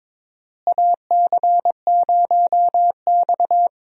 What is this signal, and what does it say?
Morse code, heard as a keyed tone, about 22 wpm, 700 Hz. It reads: AC0X
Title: USA Call Signs - 4 Characters - Repeated 3x - 22wpm